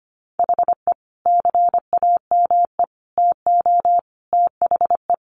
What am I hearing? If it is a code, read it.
Morse code, heard as a keyed tone, about 25 wpm, 700 Hz. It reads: HE CAME TO THE
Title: Sentences from Top 1100 Words 25wpm